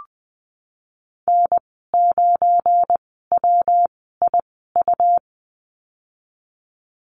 Morse code, heard as a keyed tone, about 20 wpm, spaced as for 15 wpm, 700 Hz. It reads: N9WIU
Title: 193 Lesson 31g - Letter v - Call Signs 20wpm